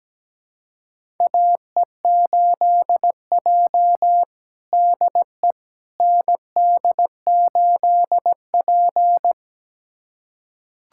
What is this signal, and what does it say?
Morse code, heard as a keyed tone, about 17 wpm, 700 Hz. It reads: AE8J DE ND8P
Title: USA Callsign DE Callsign - 4 Characters - 17wpm